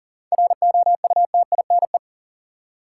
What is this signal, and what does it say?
Morse code, heard as a keyed tone, about 40 wpm, 700 Hz. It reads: ROUTINE
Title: Seven Letter Words - 40wpm